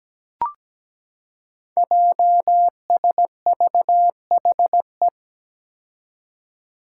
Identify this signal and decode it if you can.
Morse code, heard as a keyed tone, about 17 wpm, 700 Hz. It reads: JSVHE
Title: Five Challenging Letter-Number Combinations 17wpm